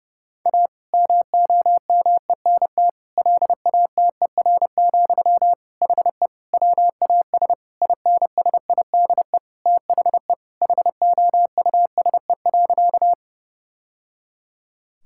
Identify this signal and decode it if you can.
Morse code, heard as a keyed tone, about 30 wpm, 700 Hz. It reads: A MOMENT LATER, HE WAS INSIDE THE HOUSE.
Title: Sentences from Top 1000 Words - Review Part 6 - 30wpm